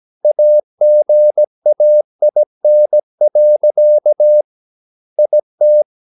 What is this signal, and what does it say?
Morse code, heard as a keyed tone, about 17 wpm, 600 Hz. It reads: AGAIN. IT
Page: Romanen Never af Ken Follet på engelsk.